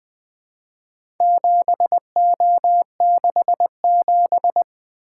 Morse code, heard as a keyed tone, about 20 wpm, 700 Hz. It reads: 7O67
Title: Four Letter-Number Combinations - 20wpm